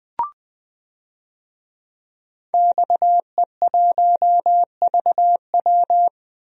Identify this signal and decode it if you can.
Morse code, heard as a keyed tone, about 20 wpm, 700 Hz. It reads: XE1VW
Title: DXCC-100 Call Signs - 5 Characters - 20wpm